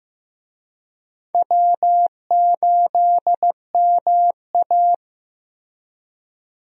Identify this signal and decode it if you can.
Morse code, heard as a keyed tone, about 16 wpm, 700 Hz. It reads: W8MA
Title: USA Call Signs - 4 Characters - Repeated 2x - 15wpm